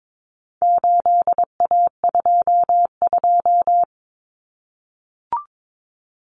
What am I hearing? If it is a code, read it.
Morse code, heard as a keyed tone, about 22 wpm, 700 Hz. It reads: 8A22
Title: Common Cut Numbers - 4-Digit 22wpm